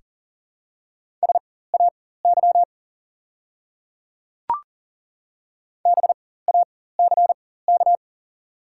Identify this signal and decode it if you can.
Morse code, heard as a keyed tone, about 40 wpm, spaced as for 19 wpm, 700 Hz. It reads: SAY BACK
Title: 4x Character Spacing - Top 300 Words - 40wpm